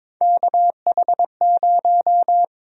Morse code, heard as a keyed tone, about 22 wpm, 700 Hz. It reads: KH0
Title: USA Call Sign Prefixes - 22wpm